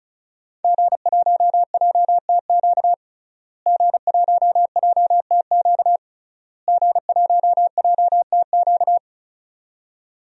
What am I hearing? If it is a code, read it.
Morse code, heard as a keyed tone, about 34 wpm, 700 Hz. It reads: G1JTQ G1JTQ G1JTQ
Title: DXCC-100 Call Signs - 5 Characters - Repeated 3x - 35wpm